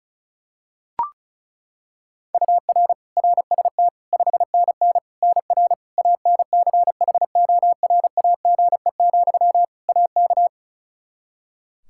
Morse code, heard as a keyed tone, about 34 wpm, 700 Hz. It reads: UR RST 5NN NR ANCHORAGE, AK
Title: Practice QSO - Exchange 1 - Beginning - 35wpm